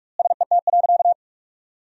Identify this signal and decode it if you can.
Morse code, heard as a keyed tone, about 42 wpm, 700 Hz. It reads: SET.
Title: 4x Word-Spacing - Sentences from Top 400 Words - 45wpm